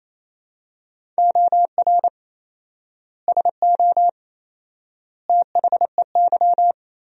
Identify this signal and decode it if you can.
Morse code, heard as a keyed tone, about 27 wpm, 700 Hz. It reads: OR SO THEY
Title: 4x Word-Spacing - Sentences from Top 100 Words - 28wpm